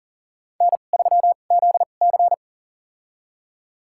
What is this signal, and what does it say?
Morse code, heard as a keyed tone, about 37 wpm, spaced as for 28 wpm, 700 Hz. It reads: N3ZC
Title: USA Call Signs - 4 Characters - 2x Character Spacing - 40wpm